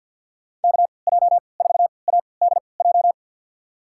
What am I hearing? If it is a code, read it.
Morse code, heard as a keyed tone, about 46 wpm, spaced as for 27 wpm, 690 Hz. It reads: KJ4ADJ
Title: USA Call Signs - 6 Characters - 3x Character Spacing - 50wpm